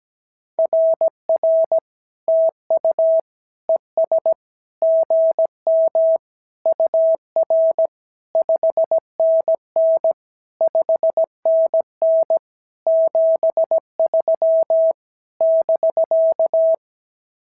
Morse code, heard as a keyed tone, about 17 wpm, 650 Hz. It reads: RR TU ES GM UR 5NN 5NN 73 <BK>
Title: SOTA QSO Practice 17wpm